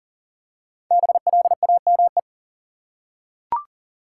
Morse code, heard as a keyed tone, about 40 wpm, 700 Hz. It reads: BLAME
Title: Five Letter Words - 40wpm